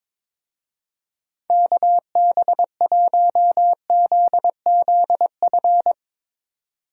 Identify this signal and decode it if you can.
Morse code, heard as a keyed tone, about 22 wpm, 700 Hz. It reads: KB1ZZF
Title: USA Call Signs - 6 Characters - 22wpm